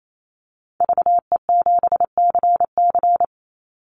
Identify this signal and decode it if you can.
Morse code, heard as a keyed tone, about 28 wpm, 700 Hz. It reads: VE7CC
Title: Call Signs - 28wpm